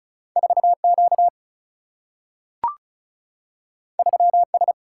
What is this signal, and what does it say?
Morse code, heard as a keyed tone, about 36 wpm, 700 Hz. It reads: 4Q 3S
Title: Single-Letter Single-Number 35wpm